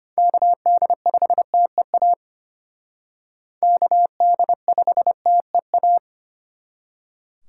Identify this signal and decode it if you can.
Morse code, heard as a keyed tone, about 33 wpm, 700 Hz. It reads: KD5TEA KD5TEA